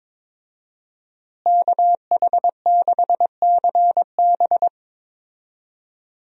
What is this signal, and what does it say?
Morse code, heard as a keyed tone, about 22 wpm, 700 Hz. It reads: KH6CB
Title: USA Call Signs - 5 Characters - Repeated 2x - 22wpm